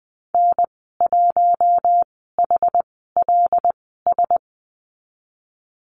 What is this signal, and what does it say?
Morse code, heard as a keyed tone, about 20 wpm, spaced as for 15 wpm, 700 Hz. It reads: N1HLS
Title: 076 Lesson 13g - Letter L - Call Signs 20wpm